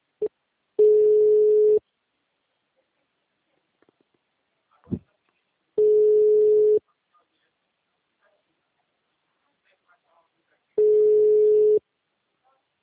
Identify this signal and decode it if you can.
Ringback tone (Europe, 425 Hz), with speech.